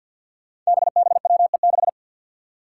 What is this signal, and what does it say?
Morse code, heard as a keyed tone, about 47 wpm, 700 Hz. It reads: BBWE6
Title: Five Challenging Letter-Number Combinations 50wpm